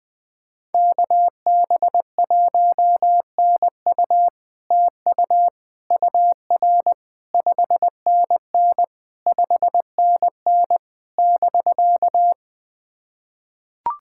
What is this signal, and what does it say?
Morse code, heard as a keyed tone, about 20 wpm, 700 Hz. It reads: KB1NU TU UR 5NN 5NN <BK>
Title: SOTA QSO Practice 20wpm